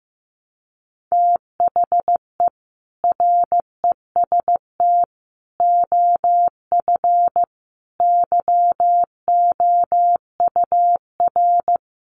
Morse code, heard as a keyed tone, about 15 wpm, 700 Hz. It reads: THE REST OF YOUR
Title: Sets of 5 Words 15wpm